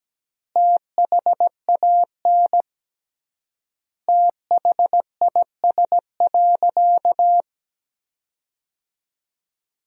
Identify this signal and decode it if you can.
Morse code, heard as a keyed tone, about 17 wpm, 700 Hz. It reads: THAN THIS.
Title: 3x Word-Spacing - Sentences from Top 100 Words - 17wpm